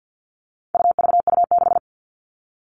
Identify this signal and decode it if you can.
Morse code, heard as a keyed tone, about 50 wpm, 700 Hz. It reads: UVU6
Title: All Cut Numbers - 4-Digit 50wpm